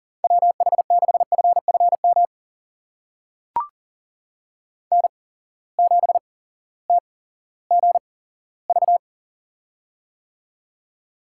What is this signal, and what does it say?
Morse code, heard as a keyed tone, about 44 wpm, spaced as for 19 wpm, 700 Hz. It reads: WH6FFM N7TGV
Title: USA Call Signs - 8x Character Spacing - 40wpm